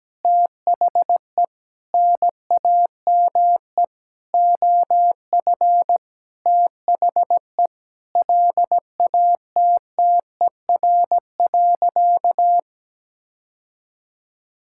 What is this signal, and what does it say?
Morse code, heard as a keyed tone, about 17 wpm, 700 Hz. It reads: THE NAME OF THE LATTER.
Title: Sentences from Top 1600 Words 17wpm